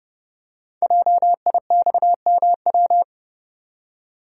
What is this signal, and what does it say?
Morse code, heard as a keyed tone, about 30 wpm, 700 Hz. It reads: JIXMW
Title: Code Groups - 5 Letters - 30wpm